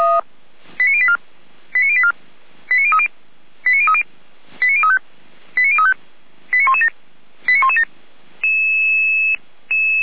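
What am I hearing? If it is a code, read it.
Five-tone selective call (ZVEI), heard as quick runs of five tones, tones of about 70 ms. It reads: 89084 89084 89030 89030 89035 89035 89117 89117
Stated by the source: ZVEI five-tone call 89092 received on an SDR.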